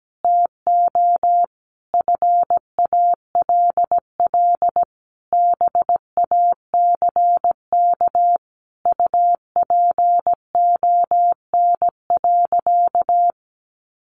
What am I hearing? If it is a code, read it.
Morse code, heard as a keyed tone, about 17 wpm, 700 Hz. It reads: TO FALL BACK UPON.
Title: Sentences from Top 1100 Words 17wpm